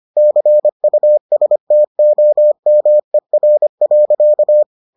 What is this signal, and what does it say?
Morse code, heard as a keyed tone, about 25 wpm, 600 Hz. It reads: CUSTOMER.